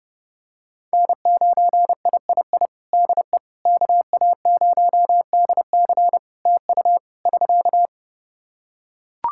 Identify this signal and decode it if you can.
Morse code, heard as a keyed tone, about 30 wpm, 700 Hz. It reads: N9III DE KA0DC TU <SK>
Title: Practice QSO - Exchange 4 - Ending - 30wpm